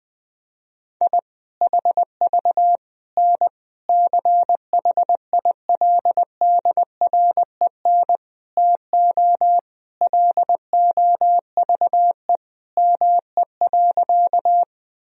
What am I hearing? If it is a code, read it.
Morse code, heard as a keyed tone, about 21 wpm, 700 Hz. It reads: I HV N CHILDREN TO LOVE ME.
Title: Sentences from Top 400 Words and Abbreviations 20wpm